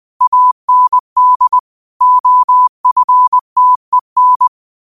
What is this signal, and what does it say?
Morse code, heard as a keyed tone, about 20 wpm, 1000 Hz. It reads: AND OFTEN
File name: qotdfiles / 20wpm